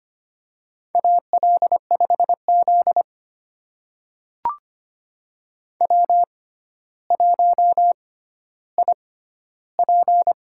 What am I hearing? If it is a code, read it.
Morse code, heard as a keyed tone, about 25 wpm, spaced as for 13 wpm, 700 Hz. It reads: AL5Z W1IP
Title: USA Call Signs - 4 Characters - 6x Character Spacing - 25wpm